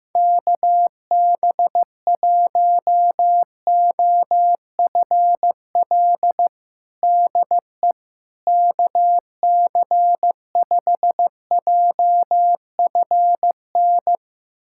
Morse code, heard as a keyed tone, about 16 wpm, 700 Hz. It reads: KB1OFL DE KC5JFN
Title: USA Callsign DE Callsign - 6 Characters - 15wpm